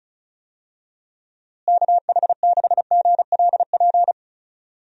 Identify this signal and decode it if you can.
Morse code, heard as a keyed tone, about 36 wpm, 700 Hz. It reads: KH6GLP
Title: DXCC-100 Call Signs - 6 Characters - Repeated 2x - 35wpm